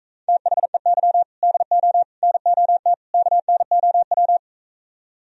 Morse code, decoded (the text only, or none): THEY DO NOT KNOW